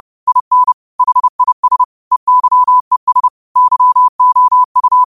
Morse code, heard as a keyed tone, about 30 wpm, 1000 Hz. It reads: IN HIS EYES YOU
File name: qotdfiles / 30wpm